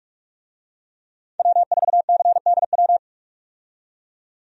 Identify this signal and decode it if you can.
Morse code, heard as a keyed tone, about 42 wpm, 700 Hz. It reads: W4CDW
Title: USA Call Signs - 5 Characters - 45wpm